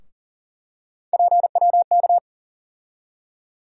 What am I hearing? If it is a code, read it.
Morse code, heard as a keyed tone, about 40 wpm, 700 Hz. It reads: PWK
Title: USA Call Sign Postfixes - 3 Letters 40wpm